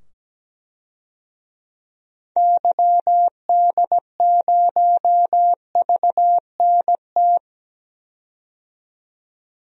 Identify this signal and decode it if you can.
Morse code, heard as a keyed tone, about 17 wpm, 700 Hz. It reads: YD0VNT
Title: DXCC-100 Call Signs - 6 Characters - 17wpm